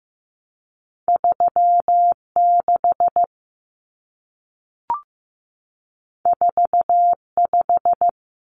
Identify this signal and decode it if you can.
Morse code, heard as a keyed tone, about 15 wpm, 700 Hz. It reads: 3 6 4 5
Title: Numbers - 2-Digit 15wpm